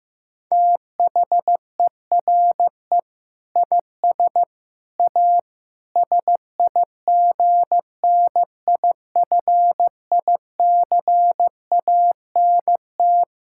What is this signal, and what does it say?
Morse code, heard as a keyed tone, about 15 wpm, 700 Hz. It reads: THERE IS A SIGNIFICANT
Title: Sentences from Top 1000 Words - Review Part 6 - 15wpm